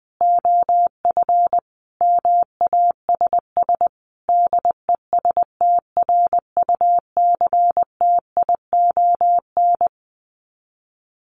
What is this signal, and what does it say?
Morse code, heard as a keyed tone, about 20 wpm, 700 Hz. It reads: OF MASS DESTRUCTION
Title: Sets of 5 Words Encore 20wpm